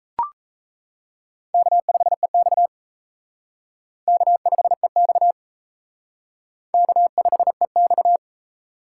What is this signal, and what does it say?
Morse code, heard as a keyed tone, about 34 wpm, 700 Hz. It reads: K5EX K5EX K5EX
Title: Speed Racing - USA 1x2 Call Signs 28wpm